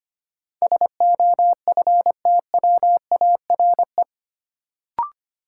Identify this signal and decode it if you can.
Morse code, heard as a keyed tone, about 25 wpm, 700 Hz. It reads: SOFTWARE
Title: Sentences from Top 1000 Words - Review Part 3 - New Word Drill - 25wpm